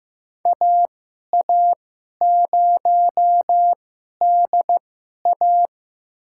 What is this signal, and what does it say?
Morse code, heard as a keyed tone, about 15 wpm, spaced as for 11 wpm, 700 Hz. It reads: AA0DA
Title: USA Call Signs - 5 Characters - 2x Character Spacing - 15wpm